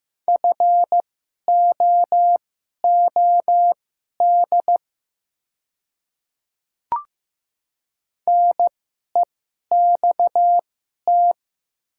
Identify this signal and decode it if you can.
Morse code, heard as a keyed tone, about 15 wpm, spaced as for 11 wpm, 700 Hz. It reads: FOOD NEXT
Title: 2x Character Spacing - Top 500 Words - 15wpm